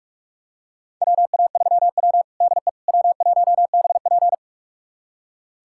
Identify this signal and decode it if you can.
Morse code, heard as a keyed tone, about 42 wpm, 700 Hz. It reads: WA3W DE W1BP
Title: USA Callsign DE Callsign - 4 Characters - 45wpm